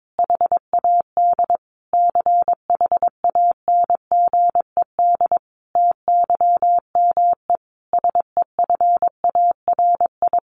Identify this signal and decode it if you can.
Morse code, heard as a keyed tone, about 22 wpm, 700 Hz. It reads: HAD CHANGED TYME SEFARI
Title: A Sound Of Thunder 22wpm